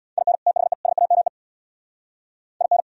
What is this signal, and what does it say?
Morse code, heard as a keyed tone, about 78 wpm, 700 Hz. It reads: IN THE SAME IN